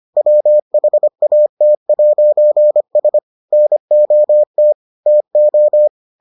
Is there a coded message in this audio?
Yes: Morse code from 0.2 to 5.9 s, heard as a keyed tone, about 25 wpm, 600 Hz. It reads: WHAT'S NOT TO